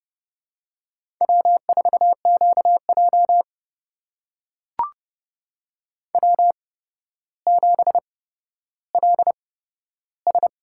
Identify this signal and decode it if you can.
Morse code, heard as a keyed tone, about 31 wpm, spaced as for 13 wpm, 700 Hz. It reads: W4QJ W7LS
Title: USA Call Signs - 4 Characters - 8x Character Spacing - 30wpm